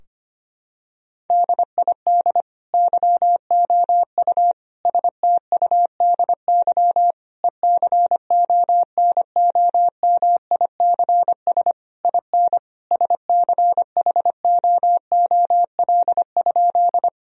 Morse code, heard as a keyed tone, about 25 wpm, 700 Hz. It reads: DID YOU STUDY ECONOMICS IN SCHOOL?
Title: Sentences from Top 1000 Words - Review Part 3 - 25wpm